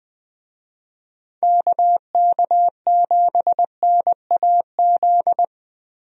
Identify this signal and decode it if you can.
Morse code, heard as a keyed tone, about 20 wpm, 700 Hz. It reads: KK7NAZ
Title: USA Call Signs - 6 Characters - Repeated 3x - 20wpm